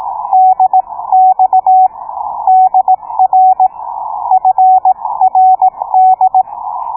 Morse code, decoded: DX DR FRD